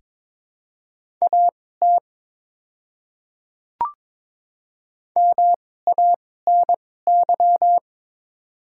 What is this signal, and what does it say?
Morse code, heard as a keyed tone, about 22 wpm, spaced as for 16 wpm, 690 Hz. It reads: AT MANY
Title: 2x Character Spacing - Top 100 Words - 22wpm